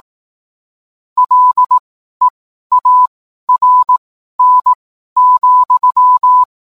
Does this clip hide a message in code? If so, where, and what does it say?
Yes: Morse code from 1.2 to 6.4 s, heard as a keyed tone, about 18 wpm, spaced as for 12 wpm, 1000 Hz. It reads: LEARN,